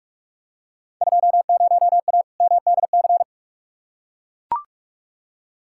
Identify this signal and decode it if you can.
Morse code, heard as a keyed tone, about 42 wpm, 700 Hz. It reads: 20A MDC
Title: Field Day Exchanges Encore - 45wpm